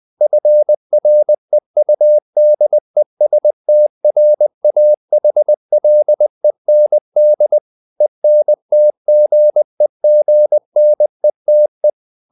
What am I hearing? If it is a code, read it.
Morse code, heard as a keyed tone, about 20 wpm, 600 Hz. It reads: FREUDESTRAHLEND ENTGEGNETE
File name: dk5ke / Humor-100BpM